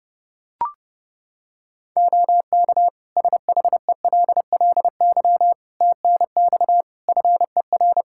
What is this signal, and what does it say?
Morse code, heard as a keyed tone, about 30 wpm, 700 Hz. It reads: OK SHELLY TNX FER
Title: Practice QSO - Exchange 4 - Ending - 30wpm